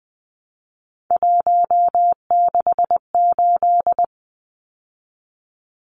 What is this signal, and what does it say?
Morse code, heard as a keyed tone, about 20 wpm, 700 Hz. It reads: 168
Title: Numbers - 3-Digit 20wpm